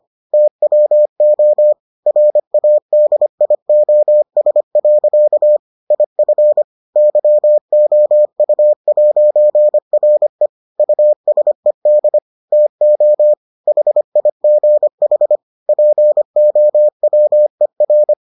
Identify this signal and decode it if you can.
Morse code, heard as a keyed tone, about 25 wpm, 600 Hz. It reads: TWO RADIOS. IF YOU'RE USED TO HIGH POWER